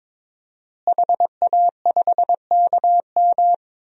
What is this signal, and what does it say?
Morse code, heard as a keyed tone, about 22 wpm, 700 Hz. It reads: HA5KM
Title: DXCC-100 Call Signs - 5 Characters - Repeated 2x - 22wpm